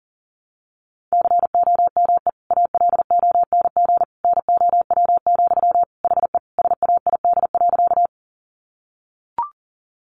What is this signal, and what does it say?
Morse code, heard as a keyed tone, about 40 wpm, 700 Hz. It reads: COME ALONG NOW, HE SAID.
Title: Sentences from Top 700 Words 40wpm